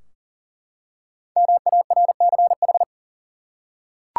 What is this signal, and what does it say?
Morse code, heard as a keyed tone, about 39 wpm, 700 Hz. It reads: MARCH
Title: Five Letter Words - 40wpm